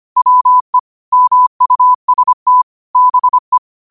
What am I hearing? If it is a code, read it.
Morse code, heard as a keyed tone, about 25 wpm, 1000 Hz. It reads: WE MUST BE
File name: qotdfiles / 25wpm